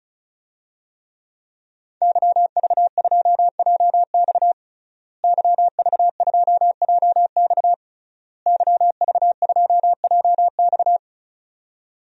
Morse code, heard as a keyed tone, about 34 wpm, 700 Hz. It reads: YV2JX YV2JX YV2JX
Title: DXCC-100 Call Signs - 5 Characters - Repeated 3x - 35wpm